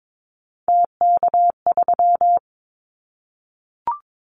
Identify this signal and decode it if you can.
Morse code, heard as a keyed tone, about 22 wpm, 700 Hz. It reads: TK3
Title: Callsign Prefixes for Top 100 DXCC - 22wpm